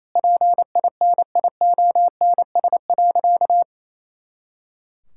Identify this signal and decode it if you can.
Morse code, heard as a keyed tone, about 27 wpm, 700 Hz. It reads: PINIONS.
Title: Pangram Challenge v2 - 28wpm